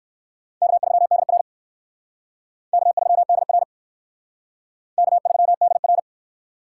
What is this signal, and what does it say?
Morse code, heard as a keyed tone, about 56 wpm, 700 Hz. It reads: K3DR K3DR K3DR
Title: Speed Racing - USA 1x2 Call Signs 45wpm